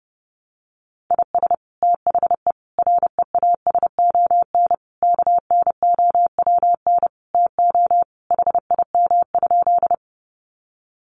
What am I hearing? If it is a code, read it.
Morse code, heard as a keyed tone, about 30 wpm, 700 Hz. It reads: IS THE REASON KNOWN TO HIM?
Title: Sentences from Top 700 Words 30wpm